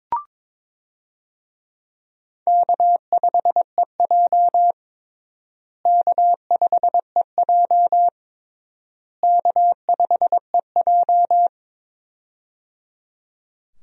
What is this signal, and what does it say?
Morse code, heard as a keyed tone, about 22 wpm, 700 Hz. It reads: K5EJ K5EJ K5EJ
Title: USA Call Signs - 4 Characters - Repeated 3x - 22wpm